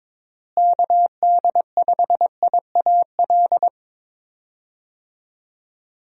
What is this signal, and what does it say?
Morse code, heard as a keyed tone, about 22 wpm, 700 Hz. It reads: KD5IAL
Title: USA Call Signs - 6 Characters - Repeated 2x - 22wpm